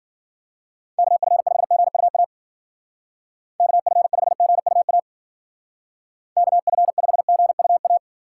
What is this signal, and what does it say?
Morse code, heard as a keyed tone, about 51 wpm, 700 Hz. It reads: KF5GUA KF5GUA KF5GUA